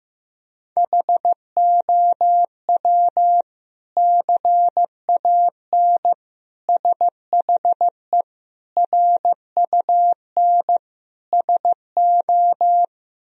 Morse code, heard as a keyed tone, about 16 wpm, 700 Hz. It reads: HOW CAN SHE RUN SO
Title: Sentences from Top 1000 Words - Review Part 2 - 15wpm